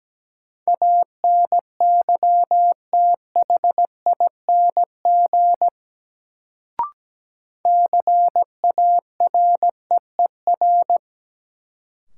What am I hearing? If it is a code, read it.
Morse code, heard as a keyed tone, about 17 wpm, 700 Hz. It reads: ANYTHING CAREER
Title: Sentences from Top 1000 Words - New Word Drill - 17wpm